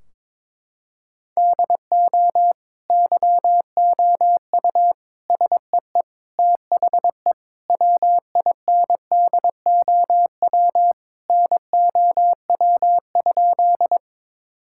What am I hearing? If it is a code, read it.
Morse code, heard as a keyed tone, about 21 wpm, 700 Hz. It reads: DO YOU SEE THE WINDOW NOW?
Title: Sentences from Top 1600 Words 22wpm